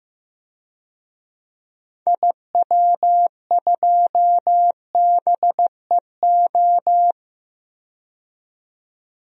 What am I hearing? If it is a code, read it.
Morse code, heard as a keyed tone, about 15 wpm, 700 Hz. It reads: IW2BEO
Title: DXCC-100 Call Signs - 6 Characters - 15wpm